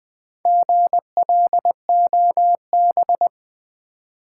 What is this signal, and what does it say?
Morse code, heard as a keyed tone, about 20 wpm, 690 Hz. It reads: GLOB
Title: Build A Word - 2 Letters At A Time - 6-Letter Words - 20wpm